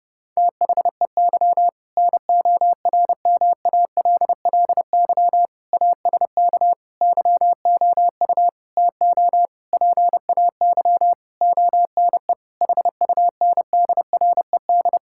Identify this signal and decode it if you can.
Morse code, heard as a keyed tone, about 30 wpm, 700 Hz. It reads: THEY NORMALLY ASK YOU TO PAY ONE HUNDRED
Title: Sentences from Top 1000 Words - Review Part 6 - 30wpm